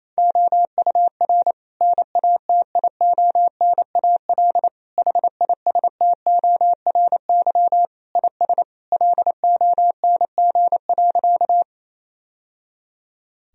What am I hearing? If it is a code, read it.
Morse code, heard as a keyed tone, about 28 wpm, 700 Hz. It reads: OUR NATIONAL HISTORY IS LONG.
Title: Sentences from Top 1000 Words - Review Part 4 - 28wpm